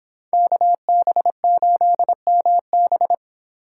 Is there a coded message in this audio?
Yes: Morse code from 0.3 to 3.1 s, heard as a keyed tone, about 30 wpm, 700 Hz. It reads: KB8MB